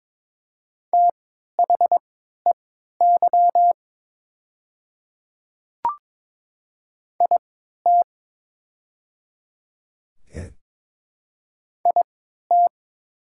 Morse code, heard as a keyed tone, about 22 wpm, spaced as for 12 wpm, 700 Hz. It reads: THEY IT IT
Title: 3x Character Spacing - Top 100 Words - 22wpm